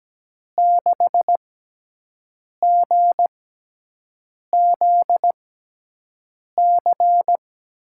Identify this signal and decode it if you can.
Morse code, heard as a keyed tone, about 17 wpm, spaced as for 6 wpm, 700 Hz. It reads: 6GZC
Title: USA Call Signs - 6 Characters - 6x Character Spacing - 17wpm